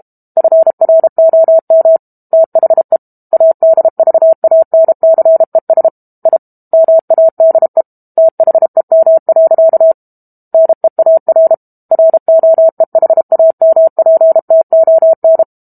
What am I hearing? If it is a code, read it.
Morse code, heard as a keyed tone, about 32 wpm, 650 Hz. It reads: FROM THE ADVANCES I MADE THEM. NEAR ROEHAMPTON
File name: Book Two / 06-Chapter 6 - 32 WPM